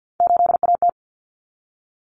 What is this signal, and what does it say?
Morse code, heard as a keyed tone, about 50 wpm, 700 Hz. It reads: 8AT
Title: Common Cut Numbers - 3-Digit 50wpm